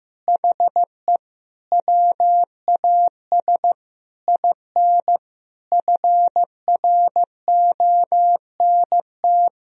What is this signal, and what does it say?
Morse code, heard as a keyed tone, about 15 wpm, 700 Hz. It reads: HE WAS IN FRONT
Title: Sentences from Top 1000 Words - Review Part 2 - 15wpm